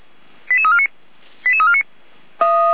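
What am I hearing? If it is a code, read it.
Five-tone selective call (ZVEI), heard as quick runs of five tones, tones of about 70 ms. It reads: 88349 88349
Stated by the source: ZVEI five-tone selective call received by SDR, Feuerwehr Orlamünde audio server.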